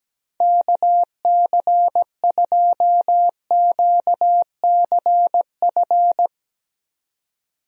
Morse code, heard as a keyed tone, about 17 wpm, 700 Hz. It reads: KC2QCF
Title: USA Call Signs - 6 Characters - 17wpm